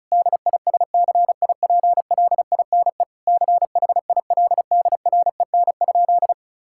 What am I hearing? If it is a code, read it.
Morse code, heard as a keyed tone, about 36 wpm, 700 Hz. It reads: DISCIPLINE CHILDREN?
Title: Sentences from Top 1000 Words - Review Part 6 - 35wpm